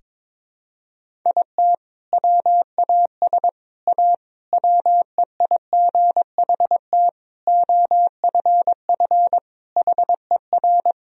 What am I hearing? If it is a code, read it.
Morse code, heard as a keyed tone, about 22 wpm, 690 Hz. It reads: IT WAS A WEIGHT OFF HER
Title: Sentences from Top 1000 Words - Review Part 3 - 22wpm